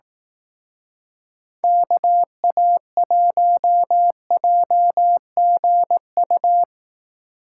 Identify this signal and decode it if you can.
Morse code, heard as a keyed tone, about 19 wpm, 700 Hz. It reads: KA1JGU
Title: Speed Racing - USA 2x3 Call Signs 15wpm